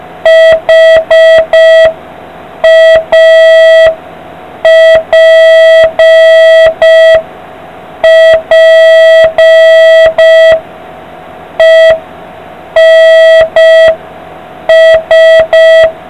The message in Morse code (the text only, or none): HAPPENS